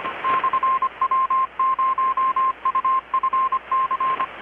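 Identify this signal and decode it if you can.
Morse code, heard as a keyed tone, about 25 wpm, 1050 Hz, with radio noise. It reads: /W0UFC
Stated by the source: Morse from KPH's Night of Nights XX maritime radiotelegraphy event, received in Chiapas, Mexico about 3,800 km away on a Kenwood R-600 with a grounded 10 m long-wire antenna.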